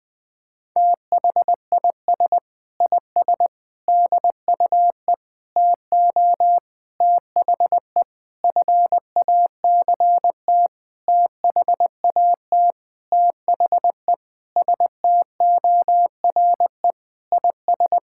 Morse code, heard as a keyed tone, about 20 wpm, 700 Hz. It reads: THIS IS DUE TO THE FACT THAT THE STORE IS
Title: Sentences with 7-Word Phrases - 20wpm